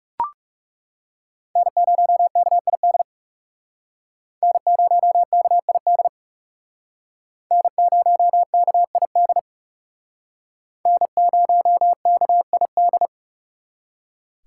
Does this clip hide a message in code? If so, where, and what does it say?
Yes: Morse code from 1.6 to 13.1 s, heard as a keyed tone, about 39 wpm, 700 Hz. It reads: N0KID N0KID N0KID N0KID